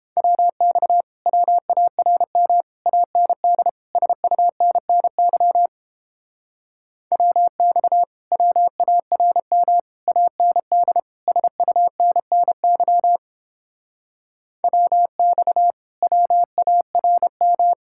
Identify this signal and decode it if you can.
Morse code, heard as a keyed tone, about 31 wpm, 700 Hz. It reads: WX WARM AND SUNNY WX WARM AND SUNNY WX WARM